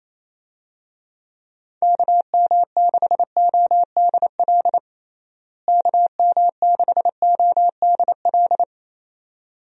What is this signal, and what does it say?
Morse code, heard as a keyed tone, about 28 wpm, 700 Hz. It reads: KM6ODL KM6ODL
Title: USA Call Signs - 6 Characters - Repeated 2x - 28wpm